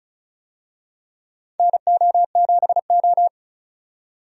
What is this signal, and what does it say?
Morse code, heard as a keyed tone, about 34 wpm, 700 Hz. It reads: NO7O
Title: USA Call Signs - 4 Characters - 35wpm